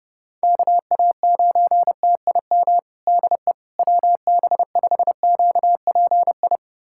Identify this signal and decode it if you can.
Morse code, heard as a keyed tone, about 30 wpm, 700 Hz. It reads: KA9TIM DE WB5QPI
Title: USA Callsign DE Callsign - 6 Characters - 30wpm